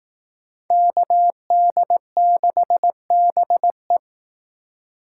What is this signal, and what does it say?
Morse code, heard as a keyed tone, about 18 wpm, 700 Hz. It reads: KD6BE
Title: Speed Racing - USA 2x2 Call Signs 15wpm